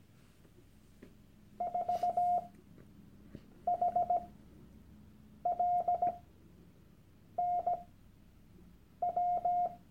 Morse code, heard as a keyed tone, about 17 wpm, spaced as for 6 wpm, 710 Hz. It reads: 4HLNW
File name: Workshop Dateien / CW-Online-Kurs / Lektion 001